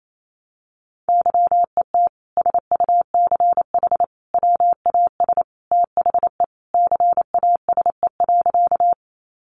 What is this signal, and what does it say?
Morse code, heard as a keyed tone, about 28 wpm, 700 Hz. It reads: YET SUCH WAS THE CASE.
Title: Sentences from Top 700 Words 28wpm